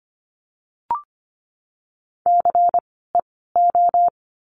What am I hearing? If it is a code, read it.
Morse code, heard as a keyed tone, about 24 wpm, 700 Hz. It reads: C E O
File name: course / 25:10wpm / 094 Lesson 16f - Letter c - Words 25wpm